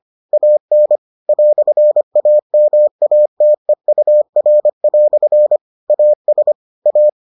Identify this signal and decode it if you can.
Morse code, heard as a keyed tone, about 25 wpm, 600 Hz. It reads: AN "AMATEUR" AS A